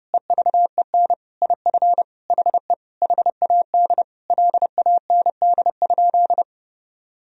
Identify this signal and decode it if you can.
Morse code, heard as a keyed tone, about 30 wpm, 700 Hz. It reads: EVEN IF HE HAD LAND?
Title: Sentences from Top 1000 Words - Review Part 1 - 30wpm